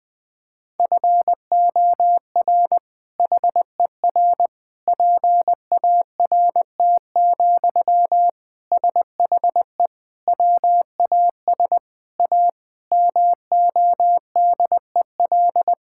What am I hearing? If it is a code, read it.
Morse code, heard as a keyed tone, about 21 wpm, 700 Hz. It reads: FOR HER PART, SHE WAS A MODEL
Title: Sentences from Top 1000 Words - Review Part 5 - 20wpm